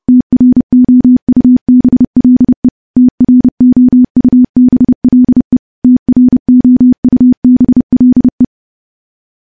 Morse code, heard as a keyed tone, about 30 wpm, 250 Hz. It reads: TROUBLE TROUBLE TROUBLE
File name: morse / morse-training-30WPM